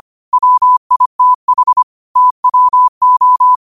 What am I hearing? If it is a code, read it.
Morse code, heard as a keyed tone, about 25 wpm, 1000 Hz. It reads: WITH TWO
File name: 25wpm